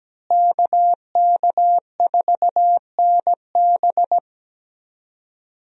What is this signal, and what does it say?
Morse code, heard as a keyed tone, about 17 wpm, 700 Hz. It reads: KK4NB
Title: USA Call Signs - 5 Characters - Repeated 3x - 17wpm